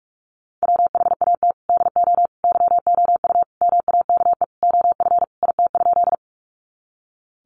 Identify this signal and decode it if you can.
Morse code, heard as a keyed tone, about 44 wpm, 700 Hz. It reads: WHAT DO YOU MAKE OF IT?
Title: Sentences from Top 100 Words 45wpm